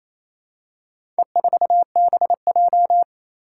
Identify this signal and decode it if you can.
Morse code, heard as a keyed tone, about 28 wpm, 700 Hz. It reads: E4BJ
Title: Four Challenging Letter-Number Combinations 28wpm